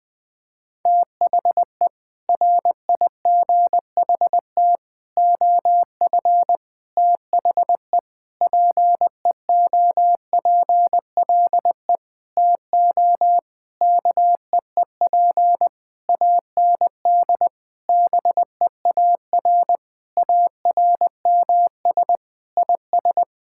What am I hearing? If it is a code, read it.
Morse code, heard as a keyed tone, about 19 wpm, 700 Hz. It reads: THE RIGHT OF THE PEOPLE TO KEEP AND BEAR ARMS IS
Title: Sentences with 9-Word Phrases - Spoken Beginning - 20wpm